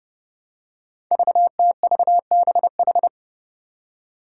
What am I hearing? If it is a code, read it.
Morse code, heard as a keyed tone, about 30 wpm, 700 Hz. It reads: VTVBH
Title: Five Challenging Letter-Number Combinations 30wpm